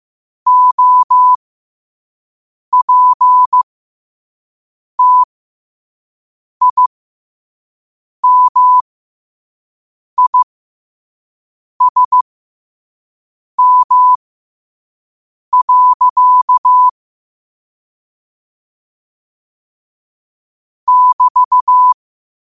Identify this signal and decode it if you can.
Morse code, heard as a keyed tone, about 15 wpm, spaced as for 5 wpm, 1000 Hz. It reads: OPTIMISM. <BT>
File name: qotdfiles / 5wpm